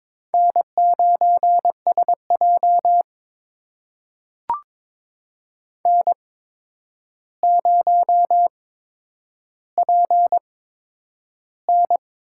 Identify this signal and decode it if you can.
Morse code, heard as a keyed tone, about 22 wpm, spaced as for 9 wpm, 700 Hz. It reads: N9SJ N0PN
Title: USA Call Signs - 4 Characters - 8x Character Spacing - 22wpm